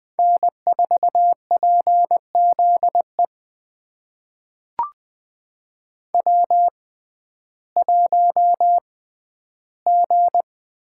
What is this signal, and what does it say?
Morse code, heard as a keyed tone, about 20 wpm, spaced as for 12 wpm, 700 Hz. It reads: N4PZE W1G
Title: USA Call Signs - 5 Characters - 6x Character Spacing - 20wpm